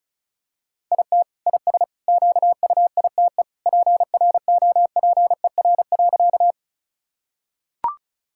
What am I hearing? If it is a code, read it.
Morse code, heard as a keyed tone, about 34 wpm, 690 Hz. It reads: IT IS QUITE PROPER.
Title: Sentences from Top 1000 Words - Review Part 5 - 35wpm